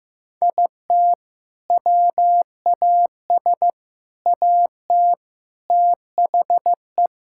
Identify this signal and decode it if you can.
Morse code, heard as a keyed tone, about 15 wpm, 700 Hz. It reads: IT WAS AT THE
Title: Sentences from Top 1600 Words 15wpm